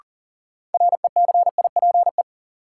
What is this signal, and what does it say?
Morse code, heard as a keyed tone, about 40 wpm, 700 Hz. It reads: RECIPE
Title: Sentences from Top 1200 Words - New Word Drill Alt - 40wpm